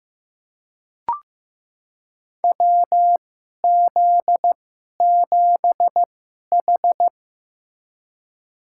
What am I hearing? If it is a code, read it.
Morse code, heard as a keyed tone, about 15 wpm, spaced as for 11 wpm, 700 Hz. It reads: WZ7H
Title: USA Call Signs - 4 Characters - 2x Character Spacing - 15wpm